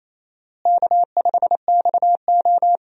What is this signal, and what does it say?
Morse code, heard as a keyed tone, about 28 wpm, 700 Hz. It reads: K5XO
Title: USA Call Signs - 4 Characters - Repeated 2x - 28wpm